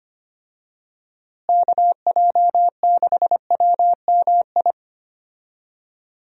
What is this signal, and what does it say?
Morse code, heard as a keyed tone, about 25 wpm, 700 Hz. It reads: KJ6WMI
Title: USA Call Signs - 6 Characters - Repeated 3x - 25wpm